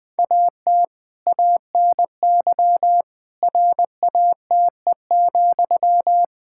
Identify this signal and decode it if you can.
Morse code, heard as a keyed tone, about 20 wpm, 700 Hz. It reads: AT ANY RATE,
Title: Sentences from Top 1600 Words 20wpm